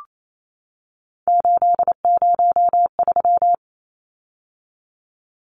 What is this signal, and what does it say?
Morse code, heard as a keyed tone, about 28 wpm, 700 Hz. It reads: 803
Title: Numbers - 3-Digit 28wpm